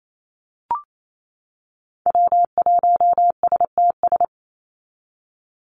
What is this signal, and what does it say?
Morse code, heard as a keyed tone, about 28 wpm, 700 Hz. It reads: W1STS
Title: USA Call Signs 28wpm